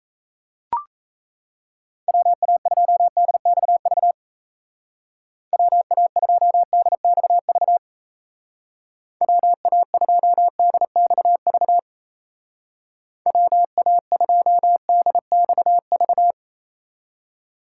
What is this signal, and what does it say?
Morse code, heard as a keyed tone, about 34 wpm, 700 Hz. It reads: WA2DXV WA2DXV WA2DXV WA2DXV
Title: Speed Racing - USA 2x3 Call Signs 28wpm